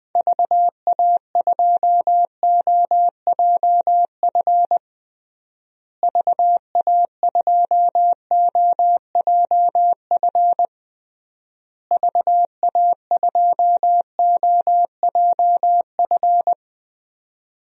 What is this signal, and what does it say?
Morse code, heard as a keyed tone, about 20 wpm, 700 Hz. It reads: VA2OJF VA2OJF VA2OJF
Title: DXCC-100 Call Signs - 6 Characters - Repeated 3x - 20wpm